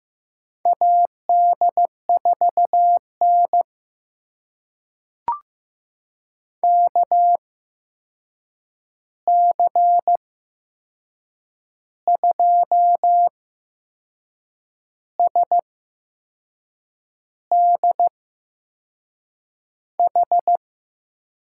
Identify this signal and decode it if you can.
Morse code, heard as a keyed tone, about 15 wpm, spaced as for 6 wpm, 700 Hz. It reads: AD4N KC2SDH
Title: USA Call Signs - 8x Character Spacing - 15wpm